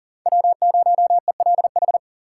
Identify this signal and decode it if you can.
Morse code, heard as a keyed tone, about 40 wpm, 700 Hz. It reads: W0ELH
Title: USA Call Signs - 6x Character Spacing - 40wpm